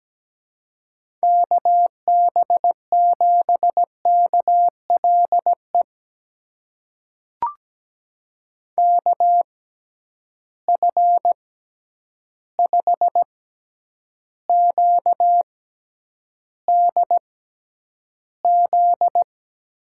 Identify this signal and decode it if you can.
Morse code, heard as a keyed tone, about 17 wpm, spaced as for 9 wpm, 700 Hz. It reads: KB7KLE KF5QDZ
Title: USA Call Signs - 6 Characters - 6x Character Spacing - 17wpm